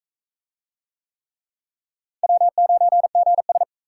Morse code, heard as a keyed tone, about 36 wpm, 700 Hz. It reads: W9GS